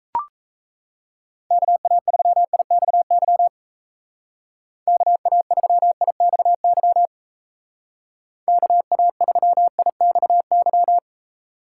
Morse code, heard as a keyed tone, about 36 wpm, 700 Hz. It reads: KA3IXY KA3IXY KA3IXY